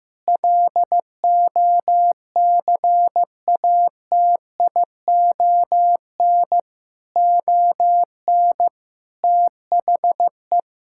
Morse code, heard as a keyed tone, about 15 wpm, 700 Hz. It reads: LOCATION ON THE
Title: Sentences from Top 1000 Words - Review Part 6 - 15wpm